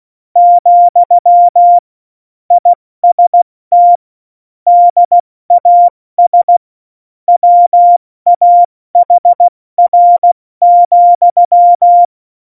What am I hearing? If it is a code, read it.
Morse code, heard as a keyed tone, about 16 wpm, 700 Hz. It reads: , IST DAS WAHR,
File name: Morsekurs / Witze / Witz 13. Februar 2018, 70 F80